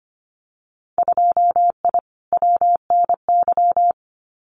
Morse code, heard as a keyed tone, about 25 wpm, 690 Hz. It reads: 2I WNY
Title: Winter Field Day Exchanges Encore 25wpm